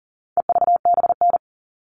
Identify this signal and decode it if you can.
Morse code, heard as a keyed tone, about 40 wpm, 700 Hz. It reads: EVBN
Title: All Cut Numbers - 4-Digit 40wpm